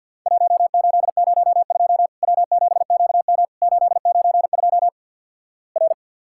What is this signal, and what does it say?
Morse code, heard as a keyed tone, about 50 wpm, 700 Hz. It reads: 1802 W7QM 892 R
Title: Worked All Europe DX Contest - QTC Exchange - 50wpm